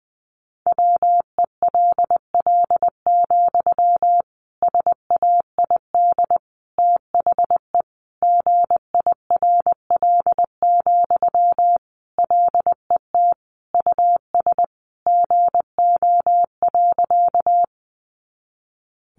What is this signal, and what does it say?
Morse code, heard as a keyed tone, about 20 wpm, 700 Hz. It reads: WELL, SAID THE GIRL, LET US GO.
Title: Sentences from Top 400 Words 20wpm